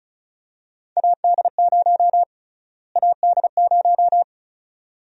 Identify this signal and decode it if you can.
Morse code, heard as a keyed tone, about 36 wpm, 700 Hz. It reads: AD0 AD0
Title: USA Call Sign Prefixes - Repeated 2x - 35wpm